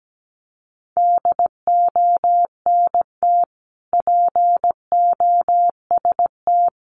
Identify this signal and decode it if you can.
Morse code, heard as a keyed tone, about 17 wpm, 700 Hz. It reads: DONT POST
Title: Sentences from Top 1000 Words 17wpm